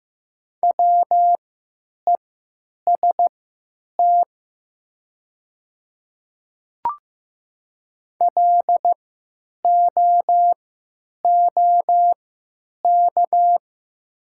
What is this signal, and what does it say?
Morse code, heard as a keyed tone, about 15 wpm, spaced as for 9 wpm, 700 Hz. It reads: WEST LOOK
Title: 3x Character Spacing - Top 500 Words - 15wpm